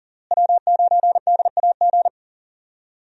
Morse code, heard as a keyed tone, about 40 wpm, 700 Hz. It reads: W9DAG
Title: USA Call Signs - 5 Characters - Repeated 3x - 40wpm